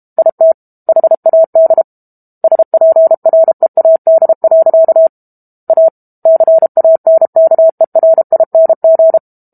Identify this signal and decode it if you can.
Morse code, heard as a keyed tone, about 32 wpm, 650 Hz. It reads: IT HAD SPREAD. A CANKERING
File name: Book Two / 06-Chapter 6 - 32 WPM